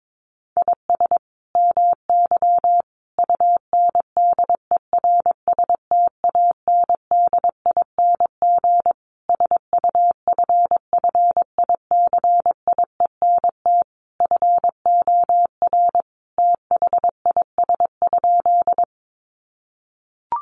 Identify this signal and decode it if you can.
Morse code, heard as a keyed tone, about 22 wpm, 700 Hz. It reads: IS MY UNDERSTANDING SUFFICIENT FOR THIS?
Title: Sentences from Top 1400 Words 22wpm